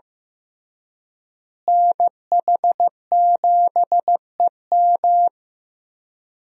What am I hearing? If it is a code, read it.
Morse code, heard as a keyed tone, about 15 wpm, 700 Hz. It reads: NH7EM
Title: USA Call Signs - 5 Characters - Repeated 3x - 15wpm